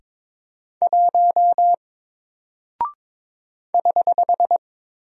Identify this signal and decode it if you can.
Morse code, heard as a keyed tone, about 22 wpm, 700 Hz. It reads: 1 <HH>
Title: Single Character (Letter:Number:Punctuation:Prosign) Alt 22wpm